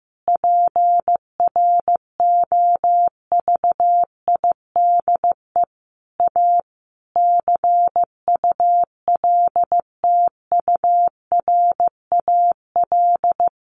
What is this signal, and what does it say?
Morse code, heard as a keyed tone, about 15 wpm, 700 Hz. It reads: PROVIDE A CULTURAL
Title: Sentences from Top 1500 Words 15wpm